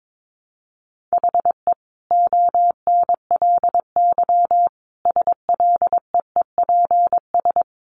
Morse code, heard as a keyed tone, about 22 wpm, 700 Hz. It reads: HE ONLY SLEEPS
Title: Additional Easy Phrases 22wpm